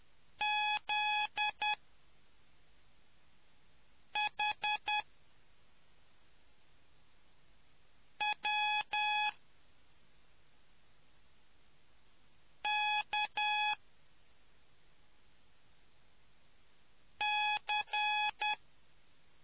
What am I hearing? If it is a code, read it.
Morse code, heard as a keyed tone, about 10 wpm, spaced as for 3 wpm, 840 Hz. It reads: ZHWKC